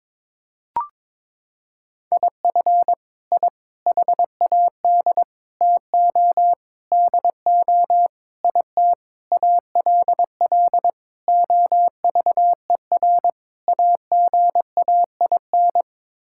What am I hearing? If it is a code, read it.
Morse code, heard as a keyed tone, about 22 wpm, 700 Hz. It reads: IF I HAD TO DO IT ALL OVER AGAIN
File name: Sets of 9 Words - 22wpm